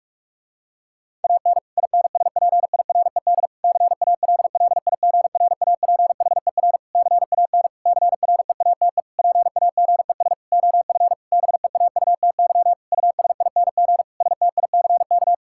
Morse code, heard as a keyed tone, about 46 wpm, 700 Hz. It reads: AN INSPIRED CALLIGRAPHER CAN CREATE PAGES OF BEAUTY USING STICK
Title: Pangram Challenge v2 - 45wpm